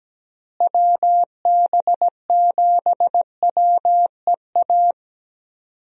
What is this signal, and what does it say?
Morse code, heard as a keyed tone, about 17 wpm, 700 Hz. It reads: WB7WEA
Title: USA Call Signs - 6 Characters - Repeated 2x - 17wpm